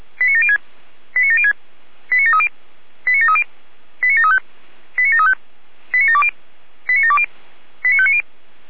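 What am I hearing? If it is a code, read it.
Five-tone selective call (ZVEI), heard as quick runs of five tones, tones of about 70 ms. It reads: 89706 89706 89730 89730 89735 89735 89720 89720 89590
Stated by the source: ZVEI five-tone selective call 89752 received by SDR.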